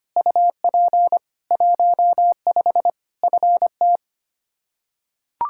Morse code, heard as a keyed tone, about 25 wpm, 700 Hz. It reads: UP 15 FT
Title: Practice QSO - Exchange 2 - Info - 25wpm